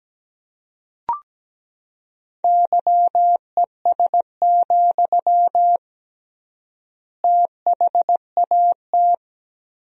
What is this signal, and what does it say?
Morse code, heard as a keyed tone, about 17 wpm, 700 Hz. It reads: YES, THAT
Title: 3x Word-Spacing - Sentences from Top 500 Words - 17wpm